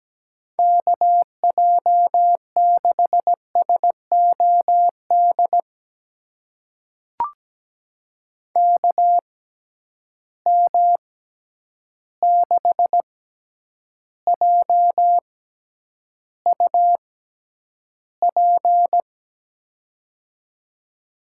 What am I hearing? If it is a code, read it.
Morse code, heard as a keyed tone, about 17 wpm, spaced as for 9 wpm, 700 Hz. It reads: KJ6SOD KM6JUP
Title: USA Call Signs - 6 Characters - 6x Character Spacing - 17wpm